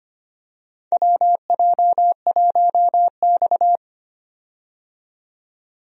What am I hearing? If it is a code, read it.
Morse code, heard as a keyed tone, about 25 wpm, 700 Hz. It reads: WJ1X
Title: USA Call Signs - 4 Characters - 25wpm